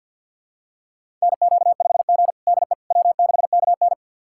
Morse code, heard as a keyed tone, about 51 wpm, 700 Hz. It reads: NQ5G DE W6KN
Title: USA Callsign DE Callsign - 4 Characters - 50wpm